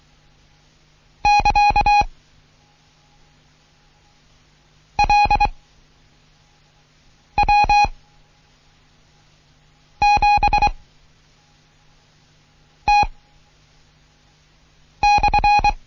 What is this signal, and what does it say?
Morse code, heard as a keyed tone, about 23 wpm, 830 Hz. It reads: <KA> L W 7 T /
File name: Morsen / Kurs A 16